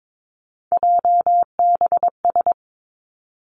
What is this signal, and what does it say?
Morse code, heard as a keyed tone, about 22 wpm, 700 Hz. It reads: JBS
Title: Three Challenging Letter-Number Combinations 22wpm